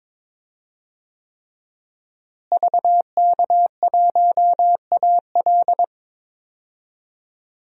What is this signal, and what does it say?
Morse code, heard as a keyed tone, about 22 wpm, 700 Hz. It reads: VK1AL
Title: DXCC-100 Call Signs - 5 Characters - Repeated 3x - 22wpm